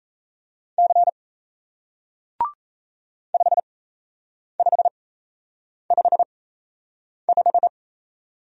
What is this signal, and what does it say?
Morse code, heard as a keyed tone, about 37 wpm, spaced as for 8 wpm, 700 Hz. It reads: C 5555